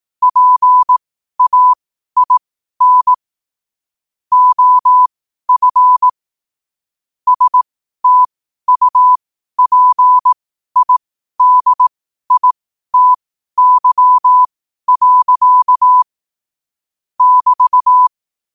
Morse code, heard as a keyed tone, about 18 wpm, spaced as for 12 wpm, 1000 Hz. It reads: PAIN OF STUPIDITY. <BT>
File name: qotdfiles / 12wpm